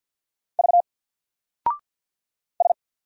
Morse code, heard as a keyed tone, about 50 wpm, 700 Hz. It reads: V S
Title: Single Character (Letter:Number:Punctuation:Prosign) 50wpm